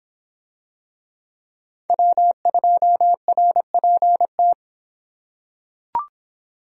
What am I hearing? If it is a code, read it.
Morse code, heard as a keyed tone, about 25 wpm, 700 Hz. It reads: W2RPT